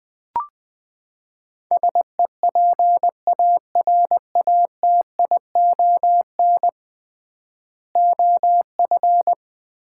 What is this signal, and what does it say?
Morse code, heard as a keyed tone, about 20 wpm, 700 Hz. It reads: SEPARATION OF
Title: 3x Word-Spacing - Sets of 5 Words - 20wpm